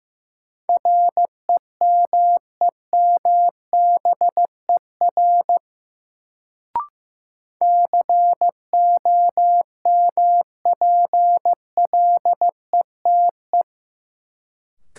Morse code, heard as a keyed tone, about 15 wpm, 700 Hz. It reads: REMEMBER COMPLETE
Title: Sentences from Top 400 Words - New Word Drill - 15wpm